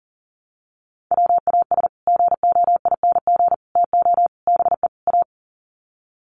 Morse code, heard as a keyed tone, about 40 wpm, 700 Hz. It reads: WAS GOING TO BE A
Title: Sets of 5 Words 40wpm